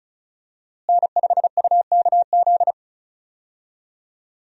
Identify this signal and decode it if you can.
Morse code, heard as a keyed tone, about 34 wpm, 700 Hz. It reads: N5UKZ
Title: USA Call Signs - 5 Characters - Repeated 2x - 35wpm